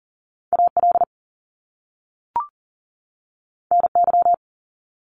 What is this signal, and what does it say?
Morse code, heard as a keyed tone, about 40 wpm, 700 Hz. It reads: AL NY
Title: Top Two Letter Combinations in Words 40wpm